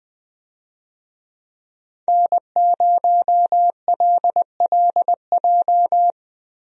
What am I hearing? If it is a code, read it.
Morse code, heard as a keyed tone, about 20 wpm, 700 Hz. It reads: N0LLJ
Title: USA Call Signs - 5 Characters - Repeated 2x - 20wpm